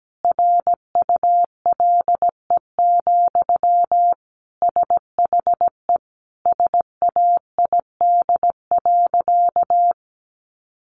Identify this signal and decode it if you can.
Morse code, heard as a keyed tone, about 17 wpm, 700 Hz. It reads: RULE, SHE SAID.
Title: Sentences from Top 400 Words 17wpm